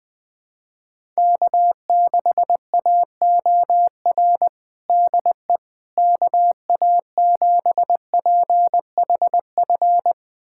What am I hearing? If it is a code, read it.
Morse code, heard as a keyed tone, about 20 wpm, 700 Hz. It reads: K6AOR DE KA7PHF
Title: USA Callsign DE Callsign 20wpm